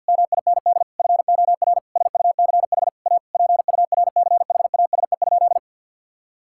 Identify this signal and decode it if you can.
Morse code, heard as a keyed tone, about 50 wpm, 700 Hz. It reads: MIND FOR SUCH A PURCHASE?
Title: Sentences from Top 1000 Words - Review Part 6 - 50wpm